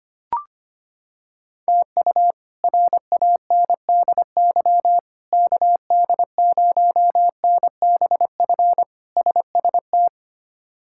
Morse code, heard as a keyed tone, about 25 wpm, 700 Hz. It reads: TU RANDY KD0NBF SST
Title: K1USN SST QSO Practice - 25wpm